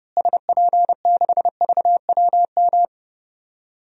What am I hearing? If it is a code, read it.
Morse code, heard as a keyed tone, about 30 wpm, 700 Hz. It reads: SP6VWM
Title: DXCC-100 Call Signs - 6 Characters - Repeated 2x - 30wpm